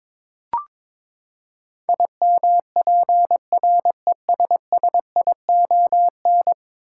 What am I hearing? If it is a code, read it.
Morse code, heard as a keyed tone, about 23 wpm, 700 Hz. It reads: IMPRESSION
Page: morse-code-ninja-practice-22wpm